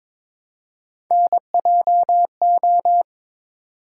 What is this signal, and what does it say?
Morse code, heard as a keyed tone, about 23 wpm, 700 Hz. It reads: NJO
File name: practice / 22wpm / Three Letter-Number Combinations - 22wpm